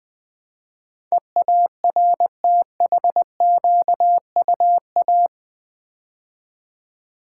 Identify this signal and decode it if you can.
Morse code, heard as a keyed tone, about 20 wpm, 700 Hz. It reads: EARTHQUA
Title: Build A Word - 4 Letters At A Time - 10-Letter Words - 20wpm